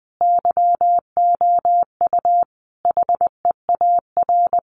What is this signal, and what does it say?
Morse code, heard as a keyed tone, about 20 wpm, 700 Hz. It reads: YOU HEAR
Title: Sentences from Top 1500 Words 20wpm